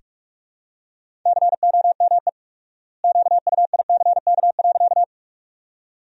Morse code, heard as a keyed tone, about 46 wpm, 700 Hz. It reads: COME QUICK.
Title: 4x Word-Spacing - Sentences from Top 500 Words - 45wpm